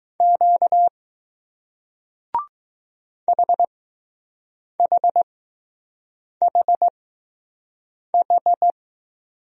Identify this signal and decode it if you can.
Morse code, heard as a keyed tone, about 20 wpm, spaced as for 6 wpm, 700 Hz. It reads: Q HHHH